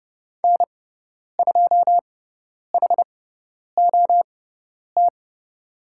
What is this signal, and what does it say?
Morse code, heard as a keyed tone, about 30 wpm, spaced as for 10 wpm, 700 Hz. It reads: N2HOT
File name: Lesson 17g - Letter u - Call Signs 30wpm